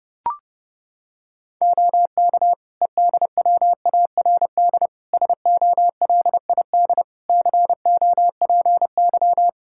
Morse code, heard as a keyed tone, about 30 wpm, 700 Hz. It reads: OK EDWARD SOLID COPY
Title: Practice QSO - Exchange 3 - More Info - 30wpm